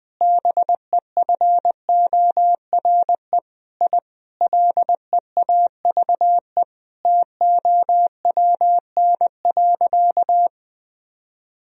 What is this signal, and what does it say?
Morse code, heard as a keyed tone, about 21 wpm, 700 Hz. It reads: BEFORE I LEAVE TOWN.
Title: Sentences from Top 1000 Words - Review Part 2 - 20wpm